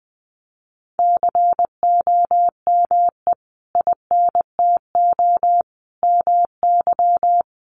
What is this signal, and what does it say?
Morse code, heard as a keyed tone, about 20 wpm, 700 Hz. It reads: COME INTO MY
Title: Sentences from Top 600 Words 20wpm